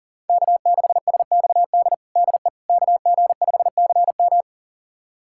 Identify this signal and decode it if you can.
Morse code, heard as a keyed tone, about 40 wpm, 700 Hz. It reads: K6SXD DE KG5CM
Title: USA Callsign DE Callsign - 5 Characters - 40wpm